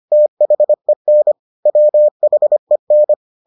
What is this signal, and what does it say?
Morse code, heard as a keyed tone, about 25 wpm, 600 Hz. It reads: THEN WHEN